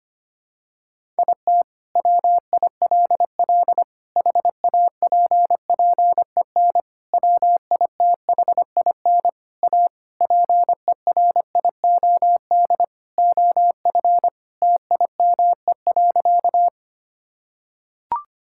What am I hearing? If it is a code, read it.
Morse code, heard as a keyed tone, about 25 wpm, 700 Hz. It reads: IT WILL HAPPEN WITHIN A PERIOD OF TIME.
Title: Sentences from Top 1000 Words - Review Part 5 - 25wpm